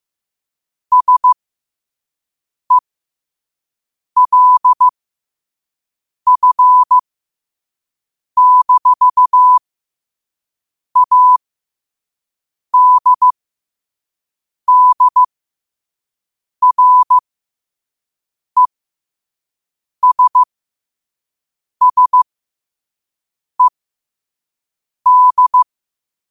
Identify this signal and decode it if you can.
Morse code, heard as a keyed tone, about 15 wpm, spaced as for 5 wpm, 1000 Hz. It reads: SELF-ADDRESSED
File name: qotdfiles / 5wpm